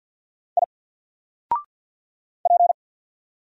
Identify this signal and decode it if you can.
Morse code, heard as a keyed tone, about 48 wpm, 700 Hz. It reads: I P
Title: Single Character (Letter:Number:Punctuation:Prosign) Alt 50wpm